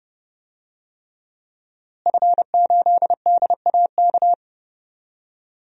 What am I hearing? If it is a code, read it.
Morse code, heard as a keyed tone, about 30 wpm, 700 Hz. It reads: F8DAK
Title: DXCC-100 Call Signs - 5 Characters - Repeated 3x - 30wpm